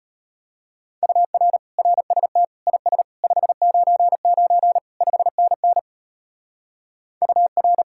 Morse code, heard as a keyed tone, about 38 wpm, 700 Hz. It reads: UR RST IS 599 5NN UR